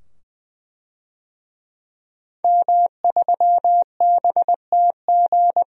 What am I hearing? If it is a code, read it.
Morse code, heard as a keyed tone, about 21 wpm, 700 Hz. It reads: M3BTG
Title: DXCC-100 Call Signs - 5 Characters - 20wpm